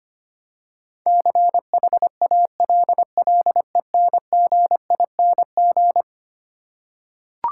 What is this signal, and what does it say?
Morse code, heard as a keyed tone, about 25 wpm, 700 Hz. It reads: CHALLENGING
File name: practice / 25wpm / Eleven Letter Words - 25wpm